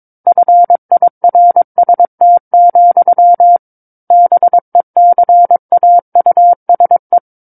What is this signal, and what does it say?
Morse code, heard as a keyed tone, about 22 wpm, 700 Hz. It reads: FIRST, BECAUSE
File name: 06-Chapter-06 22 WPM